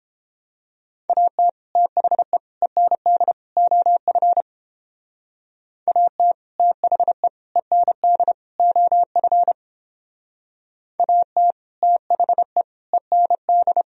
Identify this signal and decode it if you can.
Morse code, heard as a keyed tone, about 31 wpm, 700 Hz. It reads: AT THE END OF AT THE END OF AT THE END